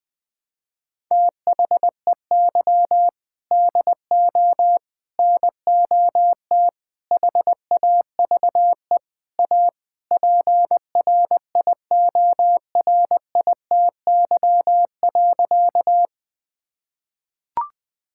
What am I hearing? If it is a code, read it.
Morse code, heard as a keyed tone, about 21 wpm, 700 Hz. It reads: THEY DO NOT HAVE A PRIORITY.
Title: Sentences from Top 1600 Words 20wpm